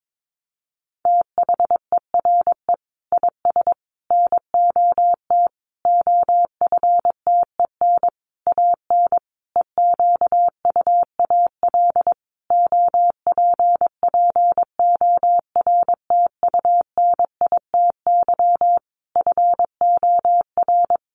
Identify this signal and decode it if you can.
Morse code, heard as a keyed tone, about 22 wpm, 700 Hz. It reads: THERE IS NOT OFTEN AN EQUAL OPPORTUNITY FOR
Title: Sentences from Top 1400 Words 22wpm